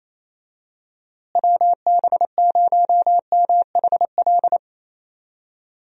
Morse code, heard as a keyed tone, about 28 wpm, 700 Hz. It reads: WB0MHL
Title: USA Call Signs - 6 Characters - Repeated 2x - 28wpm